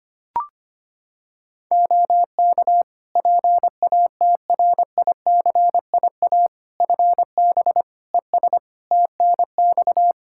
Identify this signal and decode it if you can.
Morse code, heard as a keyed tone, about 25 wpm, 700 Hz. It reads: OK PATRICIA FB ES TNX
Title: Practice QSO - Exchange 2 - Info - 25wpm